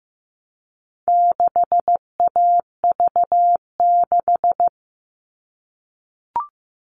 Morse code, heard as a keyed tone, about 15 wpm, 700 Hz. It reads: 6AV6
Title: All Cut Numbers - 4-Digit 15wpm